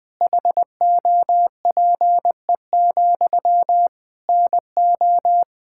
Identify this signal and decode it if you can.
Morse code, heard as a keyed tone, about 20 wpm, 700 Hz. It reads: HOPE, NO
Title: Sentences from Top 1600 Words 20wpm